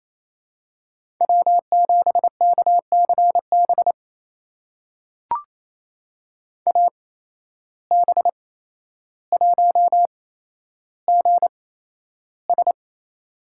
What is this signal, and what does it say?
Morse code, heard as a keyed tone, about 28 wpm, spaced as for 12 wpm, 700 Hz. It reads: W7KCB AB1GS
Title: USA Call Signs - 8x Character Spacing - 28wpm